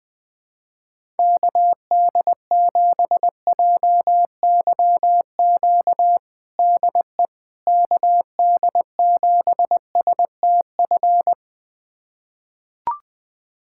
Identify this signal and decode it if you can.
Morse code, heard as a keyed tone, about 20 wpm, 700 Hz. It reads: KD7JYQ DE KD7STF
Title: USA Callsign DE Callsign 20wpm